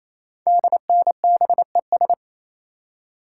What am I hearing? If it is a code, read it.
Morse code, heard as a keyed tone, about 27 wpm, 700 Hz. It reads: DNBES
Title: Code Groups - 5 Letters - 28wpm